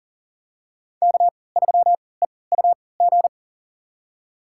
Morse code, heard as a keyed tone, about 40 wpm, spaced as for 23 wpm, 700 Hz. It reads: K3EUG
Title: USA Call Signs - 5 Characters - 3x Character Spacing - 40wpm